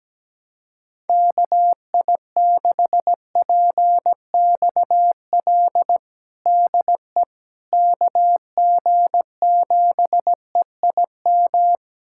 Morse code, heard as a keyed tone, about 17 wpm, 700 Hz. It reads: KI6PXL DE KG7EIM
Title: USA Callsign DE Callsign 17wpm